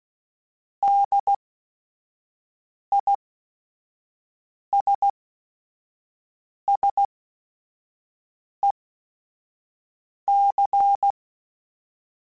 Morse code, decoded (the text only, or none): DISSEC